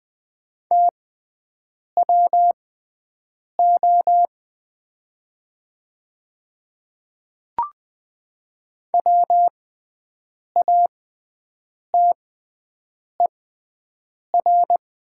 Morse code, heard as a keyed tone, about 20 wpm, spaced as for 7 wpm, 700 Hz. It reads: TWO WATER
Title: 6x Character Spacing - Top 200 Words - 20wpm